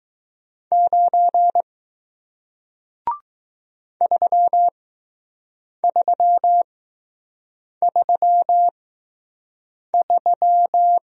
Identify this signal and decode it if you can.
Morse code, heard as a keyed tone, about 20 wpm, spaced as for 6 wpm, 700 Hz. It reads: 9 3333